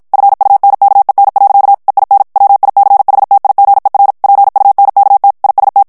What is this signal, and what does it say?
Morse code, heard as a keyed tone, about 50 wpm, 800 Hz. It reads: FUNKEN. EIN WICHTIGER PUNKT IST
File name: morse mp3 / ALLEMAND texte clair 2 vitesse 50 mots minute